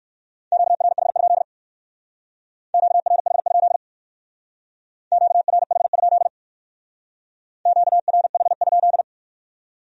Morse code, heard as a keyed tone, about 58 wpm, 700 Hz. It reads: QRH? QRH? QRH? QRH?